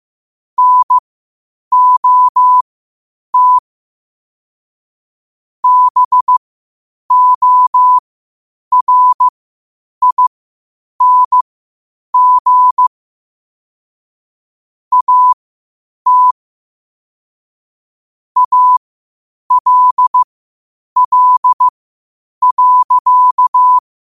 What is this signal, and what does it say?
Morse code, heard as a keyed tone, about 15 wpm, spaced as for 8 wpm, 1000 Hz. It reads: NOT BORING AT ALL.
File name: 8wpm